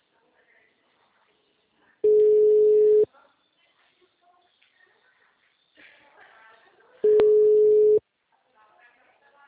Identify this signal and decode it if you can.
Ringback tone (Europe, 425 Hz), with speech.
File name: out- -2006- - - .28989